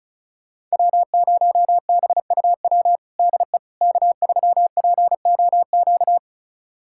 Morse code, heard as a keyed tone, about 34 wpm, 700 Hz. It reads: W0BUW DE K3POQ
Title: USA Callsign DE Callsign - 5 Characters - 35wpm